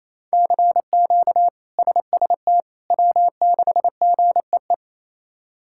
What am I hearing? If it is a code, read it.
Morse code, heard as a keyed tone, about 27 wpm, 700 Hz. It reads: CQ SST W6GEE
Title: K1USN SST QSO Practice - 28wpm